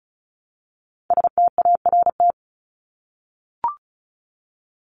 Morse code, heard as a keyed tone, about 34 wpm, 700 Hz. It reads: START
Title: Top 500 Words Repeated 35wpm